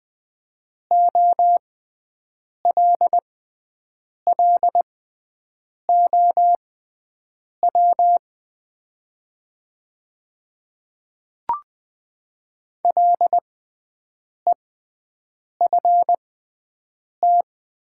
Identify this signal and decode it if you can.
Morse code, heard as a keyed tone, about 20 wpm, spaced as for 7 wpm, 700 Hz. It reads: OLLOW LEFT
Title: 6x Character Spacing - Top 300 Words - 20wpm